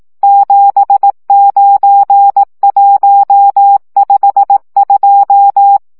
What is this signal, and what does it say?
Morse code, heard as a keyed tone, about 18 wpm, 800 Hz. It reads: 79152
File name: morze vizsga 2011-3 15